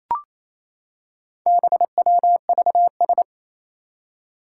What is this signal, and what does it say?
Morse code, heard as a keyed tone, about 28 wpm, 700 Hz. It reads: BWVS
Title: Four Challenging Letter-Number Combinations 28wpm